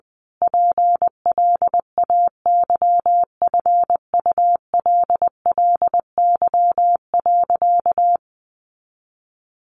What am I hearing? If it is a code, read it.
Morse code, heard as a keyed tone, about 20 wpm, 700 Hz. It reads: PLAYFULLY.
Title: A Sound Of Thunder 20wpm